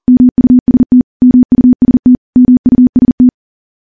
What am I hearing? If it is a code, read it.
Morse code, heard as a keyed tone, about 42 wpm, 250 Hz. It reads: MUST MUST MUST
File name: morse / morse-training-40WPM